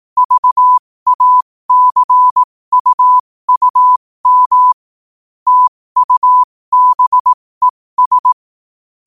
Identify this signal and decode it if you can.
Morse code, heard as a keyed tone, about 18 wpm, spaced as for 15 wpm, 1000 Hz. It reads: VACUUM TUBES
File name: qotdfiles / 15wpm